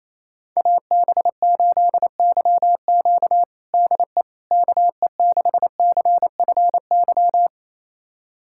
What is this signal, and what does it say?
Morse code, heard as a keyed tone, about 28 wpm, 700 Hz. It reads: AB8YQ DE KE6CFY
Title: USA Callsign DE Callsign 28wpm